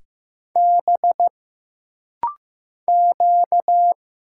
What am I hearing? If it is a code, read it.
Morse code, heard as a keyed tone, about 15 wpm, 700 Hz. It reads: B Q
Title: Single Letters - Rapid-Fire 15wpm